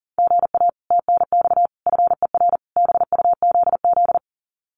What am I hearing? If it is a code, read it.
Morse code, heard as a keyed tone, about 40 wpm, 700 Hz. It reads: GA TNX FER BUZZ
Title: Common QSO Phrases Encore 40wpm